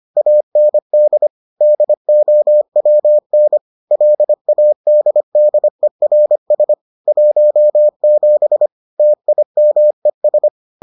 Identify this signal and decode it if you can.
Morse code, heard as a keyed tone, about 25 wpm, 600 Hz. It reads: AND DOWN LADDERS 17 TIMES